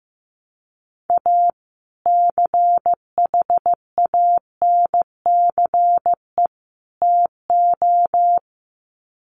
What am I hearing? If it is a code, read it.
Morse code, heard as a keyed tone, about 15 wpm, 700 Hz. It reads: A CHANCE TO
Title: Sets of 3 Words 15wpm